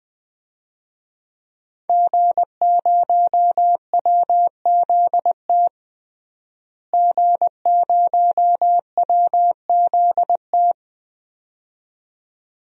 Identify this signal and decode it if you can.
Morse code, heard as a keyed tone, about 20 wpm, 700 Hz. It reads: G0WZT G0WZT
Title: DXCC-100 Call Signs - 5 Characters - Repeated 2x - 20wpm